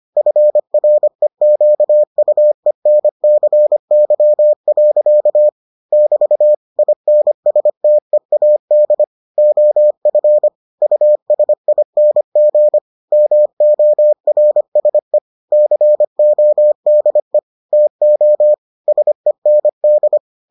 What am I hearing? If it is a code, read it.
Morse code, heard as a keyed tone, about 25 wpm, 600 Hz. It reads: FREQUENCY. <BT> INSTEAD OF USING MORSE CODE TO SEND